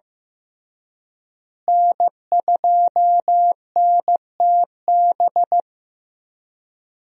Morse code, heard as a keyed tone, about 15 wpm, 700 Hz. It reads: N2NTB
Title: USA Call Signs - 5 Characters - Repeated 3x - 15wpm